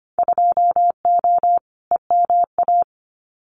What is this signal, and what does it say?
Morse code, heard as a keyed tone, about 25 wpm, 700 Hz. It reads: 2O EMA
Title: Winter Field Day Exchanges Encore 25wpm